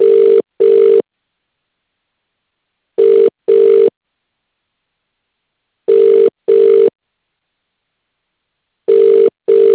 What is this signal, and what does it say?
Ringback tone (UK), 400 + 450 Hz.